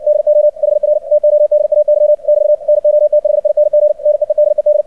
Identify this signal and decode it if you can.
Morse code, heard as a keyed tone, about 60 wpm, 600 Hz. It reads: DO HR TODAY <BT> NOTHING NEEDED
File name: archived / CRSnet